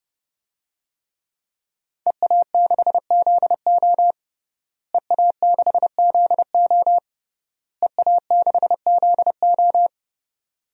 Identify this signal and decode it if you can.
Morse code, heard as a keyed tone, about 32 wpm, 700 Hz. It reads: EA6ZO EA6ZO EA6ZO
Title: DXCC-100 Call Signs - 5 Characters - Repeated 3x - 30wpm